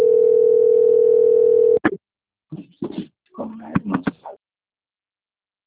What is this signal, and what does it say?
Ringback tone (North America), 440 + 480 Hz.